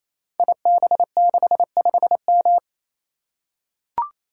Morse code, heard as a keyed tone, about 27 wpm, 700 Hz. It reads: IB65M
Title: Code Groups - 5 Letter:Numbers - 28wpm